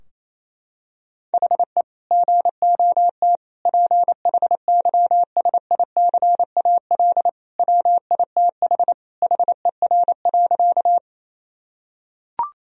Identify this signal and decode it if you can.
Morse code, heard as a keyed tone, about 28 wpm, 700 Hz. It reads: HE GOT PHYSICAL WITH HER.
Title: Sentences from Top 1000 Words - Review Part 4 - 28wpm